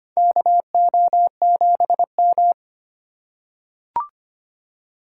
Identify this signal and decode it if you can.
Morse code, heard as a keyed tone, about 25 wpm, 700 Hz. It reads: KO7M
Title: USA Call Signs - 4 Characters - 3x Character Spacing - 25wpm